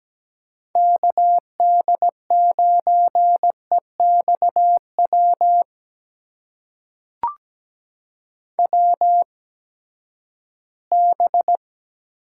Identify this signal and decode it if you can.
Morse code, heard as a keyed tone, about 17 wpm, spaced as for 12 wpm, 700 Hz. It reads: KD9EXW WB
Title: USA Call Signs - 6 Characters - 8x Character Spacing - 17wpm